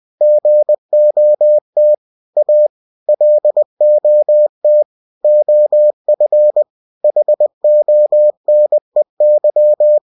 Morse code, heard as a keyed tone, about 20 wpm, 600 Hz. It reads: GOT A LOT OF HONEY